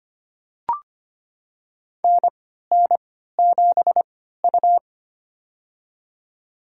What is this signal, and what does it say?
Morse code, heard as a keyed tone, about 25 wpm, spaced as for 14 wpm, 700 Hz. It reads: NN7U
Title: USA Call Signs - 4 Characters - 3x Character Spacing - 25wpm